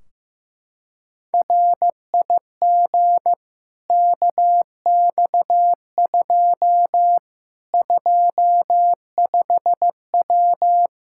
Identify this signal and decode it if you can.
Morse code, heard as a keyed tone, about 15 wpm, 700 Hz. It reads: RIG KX2 25W
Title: K1USN SST QSO Practice 15wpm